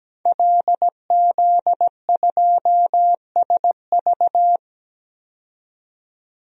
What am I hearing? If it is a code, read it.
Morse code, heard as a keyed tone, about 17 wpm, 700 Hz. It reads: LZ2SV
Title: DXCC-100 Call Signs - 5 Characters - 17wpm